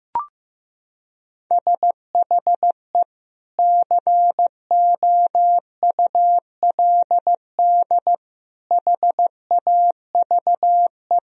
Morse code, heard as a keyed tone, about 15 wpm, 700 Hz. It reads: SHE COULD HAVE
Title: Sentences from Top 1000 Words - Review Part 1 - 15wpm